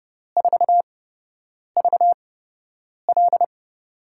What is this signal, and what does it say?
Morse code, heard as a keyed tone, about 30 wpm, spaced as for 8 wpm, 700 Hz. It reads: 4VL
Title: USA Call Signs - 4 Characters - 8x Character Spacing - 30wpm